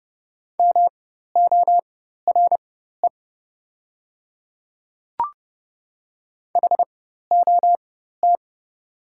Morse code, heard as a keyed tone, about 29 wpm, spaced as for 14 wpm, 700 Hz. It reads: MORE HOT
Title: 4x Character Spacing - Top 500 Words - 30wpm